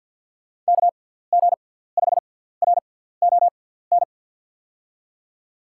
Morse code, heard as a keyed tone, about 49 wpm, spaced as for 17 wpm, 690 Hz. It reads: KG5RON
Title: USA Call Signs - 6x Character Spacing - 50wpm